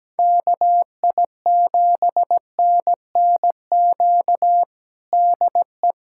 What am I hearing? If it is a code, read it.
Morse code, heard as a keyed tone, about 17 wpm, 700 Hz. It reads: KI7NNQ DE
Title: USA Callsign DE Callsign - 6 Characters - 17wpm